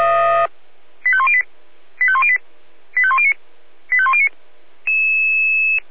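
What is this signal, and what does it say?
Five-tone selective call (ZVEI), heard as quick runs of five tones, tones of about 70 ms. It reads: 85108 85108 85109 85109
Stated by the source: ZVEI five-tone selective call 85105 received with an SDR.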